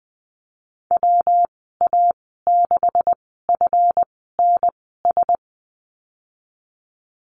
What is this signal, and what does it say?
Morse code, heard as a keyed tone, about 20 wpm, spaced as for 15 wpm, 700 Hz. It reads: WA6FNS
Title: 154 Lesson 25g - Letter p - Call Signs 20wpm